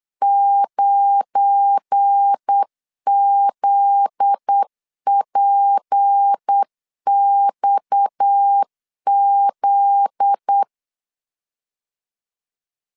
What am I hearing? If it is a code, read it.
Morse code, heard as a keyed tone, about 8 wpm, 800 Hz. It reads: 9ZPXZ
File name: CW / DET / 1 9 1500C S C 8wpm 5653